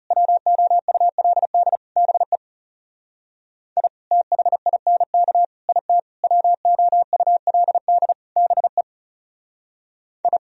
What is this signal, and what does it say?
Morse code, heard as a keyed tone, about 36 wpm, 700 Hz. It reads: WOULD BE I THINK IT WOULD BE I
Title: Speed-Racing - Sets of 5 Words 30wpm